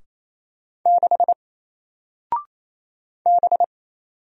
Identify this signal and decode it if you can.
Morse code, heard as a keyed tone, about 28 wpm, 700 Hz. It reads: 6 B
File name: Single-Letter-Number-28wpm